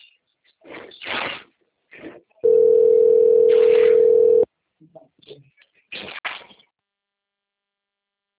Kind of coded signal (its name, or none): telephone call progress tone